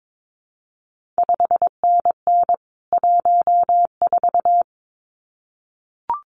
Morse code, heard as a keyed tone, about 22 wpm, 700 Hz. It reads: 5NN 14
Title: CQ WW DX Contest Exchange - 22wpm